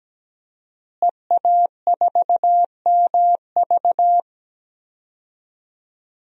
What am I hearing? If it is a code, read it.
Morse code, heard as a keyed tone, about 17 wpm, 700 Hz. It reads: EA4MV
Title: DXCC-100 Call Signs - 5 Characters - Repeated 3x - 17wpm